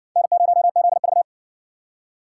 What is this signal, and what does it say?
Morse code, heard as a keyed tone, about 53 wpm, 700 Hz. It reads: N0ZV